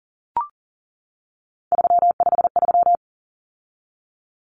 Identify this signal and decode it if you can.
Morse code, heard as a keyed tone, about 38 wpm, 700 Hz. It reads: 353
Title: Common Cut Numbers - 3-Digit 40wpm